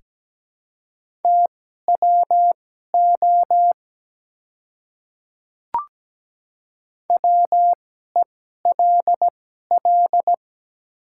Morse code, heard as a keyed tone, about 17 wpm, spaced as for 12 wpm, 700 Hz. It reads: TWO WELL
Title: 2x Character Spacing - Top 300 Words - 17wpm